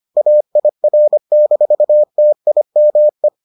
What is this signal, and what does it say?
Morse code, heard as a keyed tone, about 25 wpm, 600 Hz. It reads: AIR-TIME